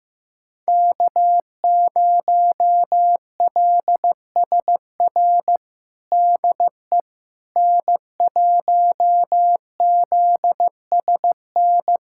Morse code, heard as a keyed tone, about 15 wpm, 700 Hz. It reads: K0LSR DE N1ZSN
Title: USA Callsign DE Callsign - 5 Characters - 15wpm